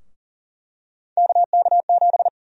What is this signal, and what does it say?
Morse code, heard as a keyed tone, about 40 wpm, 700 Hz. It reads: KK7
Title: USA Call Sign Prefixes - Repeated 2x - 40wpm